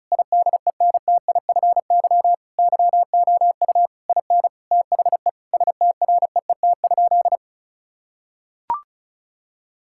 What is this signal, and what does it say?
Morse code, heard as a keyed tone, about 34 wpm, 700 Hz. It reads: IDENTIFY YOU IN THE STREET?
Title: Sentences from Top 1000 Words - Review Part 6 - 35wpm